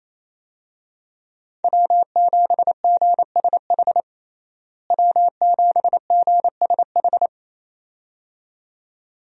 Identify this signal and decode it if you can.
Morse code, heard as a keyed tone, about 27 wpm, 700 Hz. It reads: W7GSH W7GSH
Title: USA Call Signs - 5 Characters - Repeated 2x - 28wpm